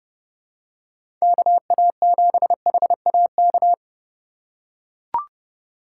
Morse code, heard as a keyed tone, about 32 wpm, 700 Hz. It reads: KA7HAK
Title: USA Call Signs - 6 Characters - 2x Character Spacing - 30wpm